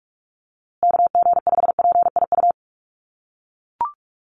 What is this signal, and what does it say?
Morse code, heard as a keyed tone, about 44 wpm, 700 Hz. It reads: KG5PIU
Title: USA Call Signs Repeated 3x 45wpm